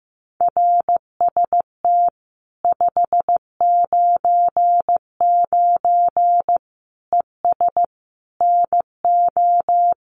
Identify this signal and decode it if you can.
Morse code, heard as a keyed tone, about 15 wpm, 700 Hz. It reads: RST 599 ES NO
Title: Common QSO Phrases 15wpm